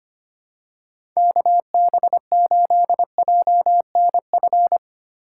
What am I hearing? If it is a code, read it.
Morse code, heard as a keyed tone, about 25 wpm, 700 Hz. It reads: KB8JNF
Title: USA Call Signs - 6 Characters - 4x Character Spacing - 25wpm